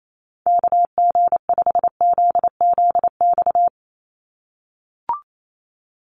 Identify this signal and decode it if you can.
Morse code, heard as a keyed tone, about 28 wpm, 700 Hz. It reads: KG5ZZX
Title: USA Call Signs 28wpm